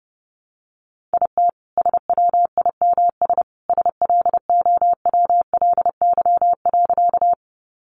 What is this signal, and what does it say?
Morse code, heard as a keyed tone, about 30 wpm, 700 Hz. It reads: IT SWIMS SLOWLY.
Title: Additional Easy Phrases 30wpm